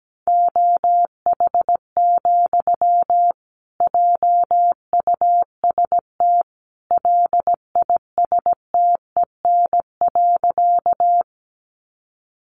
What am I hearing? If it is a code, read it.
Morse code, heard as a keyed tone, about 17 wpm, 700 Hz. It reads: OH, JUST LISTEN.
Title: Sentences from Top 500 Words 17wpm